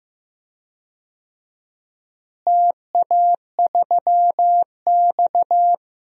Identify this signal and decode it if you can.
Morse code, heard as a keyed tone, about 15 wpm, 700 Hz. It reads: TA3X
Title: DXCC-100 Call Signs - 4 Characters - Repeated 3x - 15wpm